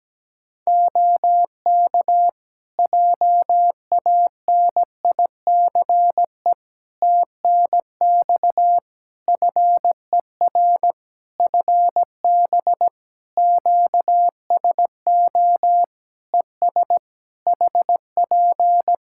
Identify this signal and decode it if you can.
Morse code, heard as a keyed tone, about 17 wpm, 700 Hz. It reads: OK JANICE TNX FER FB QSO ES HP
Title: K1USN SST QSO Practice 17wpm